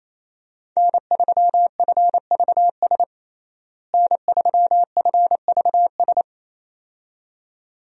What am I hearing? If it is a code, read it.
Morse code, heard as a keyed tone, about 28 wpm, 700 Hz. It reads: N3FVS N3FVS
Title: USA Call Signs - 5 Characters - Repeated 2x - 28wpm